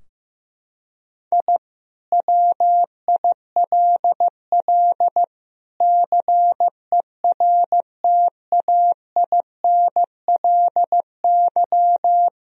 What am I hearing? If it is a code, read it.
Morse code, heard as a keyed tone, about 15 wpm, 700 Hz. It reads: I WILL CERTAINLY
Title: Sentences from Top 1000 Words - Review Part 4 - 15wpm